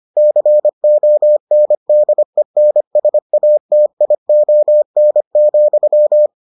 Morse code, heard as a keyed tone, about 25 wpm, 600 Hz. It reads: CONDENSATION,